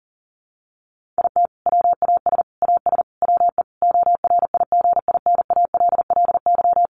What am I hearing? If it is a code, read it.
Morse code, heard as a keyed tone, about 37 wpm, 700 Hz. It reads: IT WAS AS WE ORIGINALLY
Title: Sentences from Top 1500 Words 40wpm